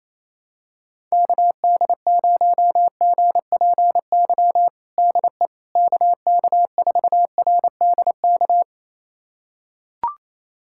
Morse code, heard as a keyed tone, about 28 wpm, 700 Hz. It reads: KD0GPY DE KK4RDK
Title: USA Callsign DE Callsign 28wpm